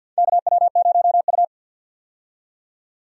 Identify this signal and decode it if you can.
Morse code, heard as a keyed tone, about 49 wpm, 700 Hz. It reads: KW0U
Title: USA Call Signs - 4 Characters - 50wpm